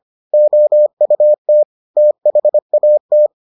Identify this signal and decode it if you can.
Morse code, heard as a keyed tone, about 25 wpm, 600 Hz. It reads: OUT THAT